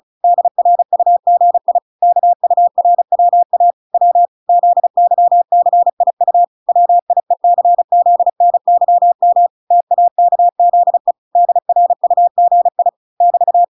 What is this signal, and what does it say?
Morse code, heard as a keyed tone, about 34 wpm, 700 Hz. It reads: DRUGI KURWA W ZYCIU WIECZNYM TAKZE DRUGI <BT>